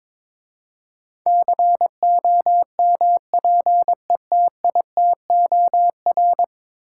Morse code, heard as a keyed tone, about 21 wpm, 700 Hz. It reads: COMPETITOR
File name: practice / 22wpm / 10 Letter Words - First 3 Letters Given - 22wpm